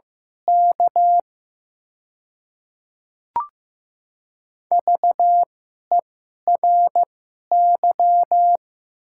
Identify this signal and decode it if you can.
Morse code, heard as a keyed tone, about 15 wpm, spaced as for 11 wpm, 700 Hz. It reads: K VERY
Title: 2x Character Spacing - Top 500 Words - 15wpm